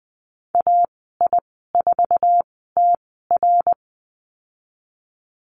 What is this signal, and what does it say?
Morse code, heard as a keyed tone, about 20 wpm, spaced as for 15 wpm, 700 Hz. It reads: AI4TR
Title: 055 Lesson 10g - Letter r - Call Signs 20wpm